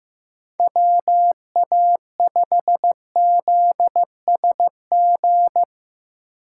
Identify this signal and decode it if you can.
Morse code, heard as a keyed tone, about 16 wpm, 700 Hz. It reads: WA5ZSG
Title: USA Call Signs - 6 Characters - 2x Character Spacing - 15wpm